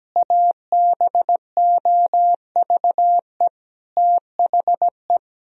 Morse code, heard as a keyed tone, about 17 wpm, 700 Hz. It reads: ABOVE THE
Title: Sentences from Top 1000 Words - Review Part 3 - 17wpm